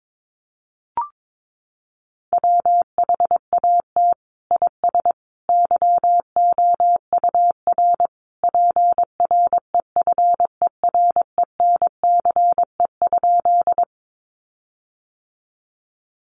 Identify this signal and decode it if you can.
Morse code, heard as a keyed tone, about 22 wpm, 700 Hz. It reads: WHAT IS YOUR PREFERENCE?
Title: Sentences from Top 1500 Words 22wpm